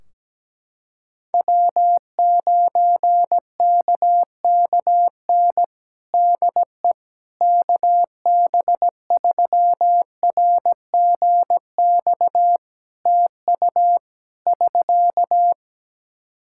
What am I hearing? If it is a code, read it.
Morse code, heard as a keyed tone, about 17 wpm, 700 Hz. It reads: W9KKN DE KB3RGX TU <SK>
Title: Practice QSO - Exchange 4 - Ending - 17wpm